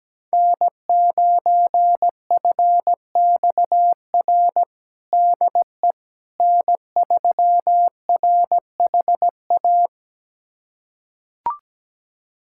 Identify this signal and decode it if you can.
Morse code, heard as a keyed tone, about 17 wpm, 700 Hz. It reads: N9FXR DE N3RHA
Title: USA Callsign DE Callsign - 5 Characters - 17wpm